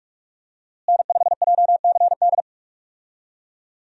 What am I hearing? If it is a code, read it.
Morse code, heard as a keyed tone, about 46 wpm, 700 Hz. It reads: N5JCD
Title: USA Call Signs - 5 Characters - Repeated 2x - 45wpm